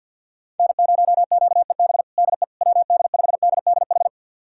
Morse code, heard as a keyed tone, about 47 wpm, 700 Hz. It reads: N0QEB DE WD5DDH
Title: USA Callsign DE Callsign 50wpm